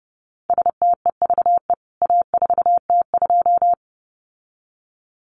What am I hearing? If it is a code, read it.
Morse code, heard as a keyed tone, about 30 wpm, 700 Hz. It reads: STEVE A4T2
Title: CWT Exchange 30wpm